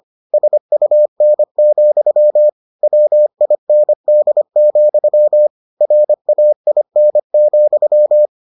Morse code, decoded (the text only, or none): SUN, WIND, RAIN,